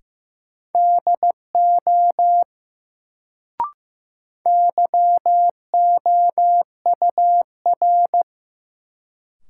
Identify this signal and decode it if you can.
Morse code, heard as a keyed tone, about 15 wpm, 700 Hz. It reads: DO YOUR
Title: Sentences from Top 100 Words - New Word Drill - 15wpm